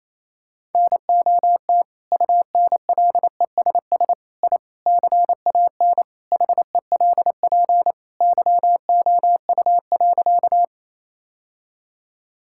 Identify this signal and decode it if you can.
Morse code, heard as a keyed tone, about 28 wpm, 700 Hz. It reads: NOT UNLESS I CAN HELP YOU.
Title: Sentences from Top 1000 Words - Review Part 6 - 28wpm